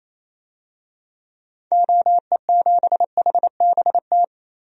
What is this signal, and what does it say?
Morse code, heard as a keyed tone, about 28 wpm, 700 Hz. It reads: OE7HBT
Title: DXCC-100 Call Signs - 6 Characters - 28wpm